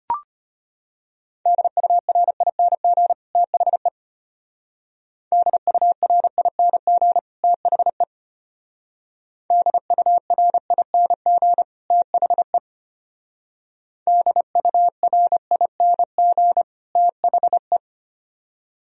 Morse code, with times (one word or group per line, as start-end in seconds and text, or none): 1.5-3.1 DURING
3.4-3.9 THE
5.3-7.2 DURING
7.4-8.0 THE
9.5-11.6 DURING
11.9-12.6 THE
14.1-16.6 DURING
17.0-17.8 THE